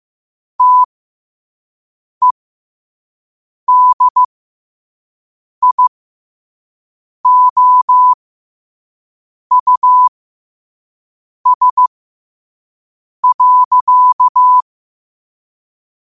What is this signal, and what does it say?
Morse code, heard as a keyed tone, about 15 wpm, spaced as for 5 wpm, 1000 Hz. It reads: TEDIOUS.
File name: qotdfiles / 5wpm